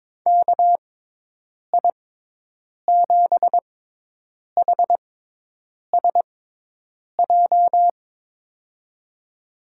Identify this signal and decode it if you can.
Morse code, heard as a keyed tone, about 21 wpm, spaced as for 8 wpm, 700 Hz. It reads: KI7HSJ
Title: USA Call Signs - 6 Characters - 6x Character Spacing - 22wpm